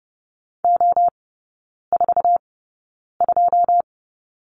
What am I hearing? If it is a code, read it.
Morse code, heard as a keyed tone, about 30 wpm, spaced as for 9 wpm, 700 Hz. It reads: O42
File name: 081 Lesson 14e - Number 2 - Three Characters 30wpm